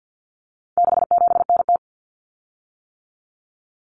Morse code, heard as a keyed tone, about 50 wpm, 700 Hz. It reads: 67NT
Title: Common Cut Numbers - 4-Digit 50wpm